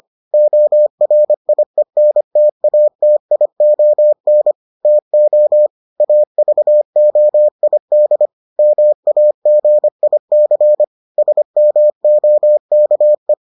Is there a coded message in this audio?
Yes: Morse code from 0.3 to 13.3 s, heard as a keyed tone, about 25 wpm, 600 Hz. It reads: ORIENTATION TO AVOID MAGIC SMOKE